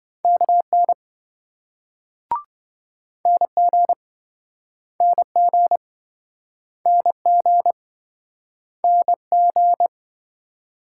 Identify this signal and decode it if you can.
Morse code, heard as a keyed tone, about 28 wpm, 700 Hz. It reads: KN NG NG NG NG